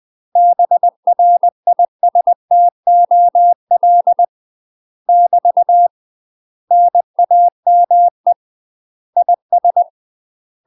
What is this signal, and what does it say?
Morse code, heard as a keyed tone, about 20 wpm, 700 Hz. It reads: BRISTOL <BT> NAME IS
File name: UK MASTER QSOs TRAINING / 20WPM / UK MASTER QSOs 0060